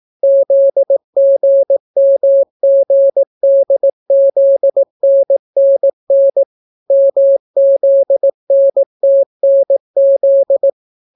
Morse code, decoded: ZGMGDZNNN MZNTNZ

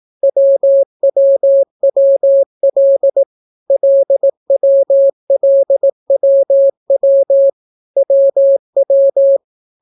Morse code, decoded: WWWL LWLWW WW

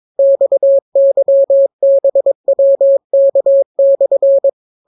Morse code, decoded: XYBWK/

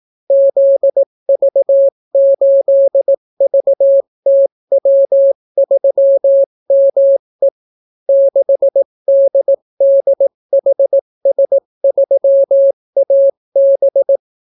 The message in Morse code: ZV8VTW3ME 6DDHS3AB

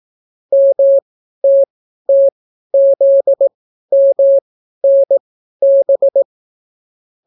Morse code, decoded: MTTZMNB

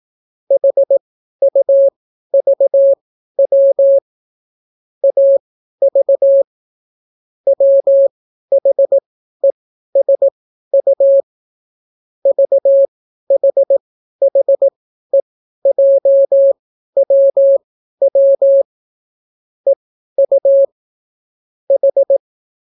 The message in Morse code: HUVW AV WHESU VHHEJWW EU H

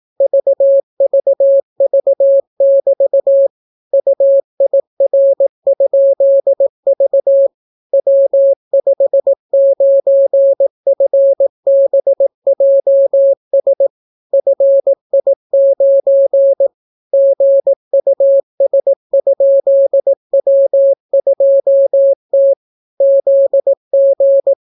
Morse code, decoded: VVV<BT> UIR?V W59FBJS FI9 GUS?W2T ZG

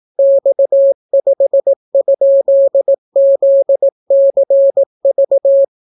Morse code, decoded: X5?ZCV

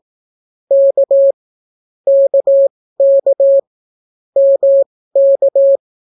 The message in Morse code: K KK MK

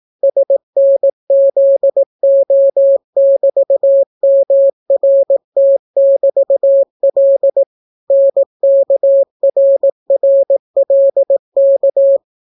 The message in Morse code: SNZO<BT>MRT<BT>L NKRRLK

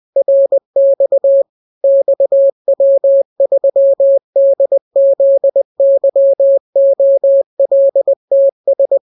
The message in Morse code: RX XW3DZYOLTS

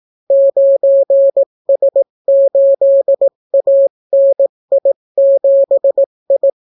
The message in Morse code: 9S8ANI7I

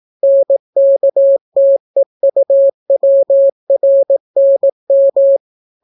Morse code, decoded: NKTEUWRNM

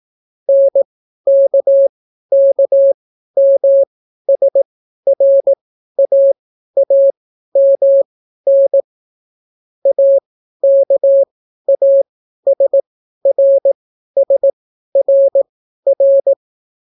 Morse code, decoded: NKKMSRAAMN AKASRSRR

